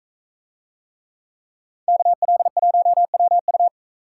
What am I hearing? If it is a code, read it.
Morse code, heard as a keyed tone, about 44 wpm, 700 Hz. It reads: KL1WU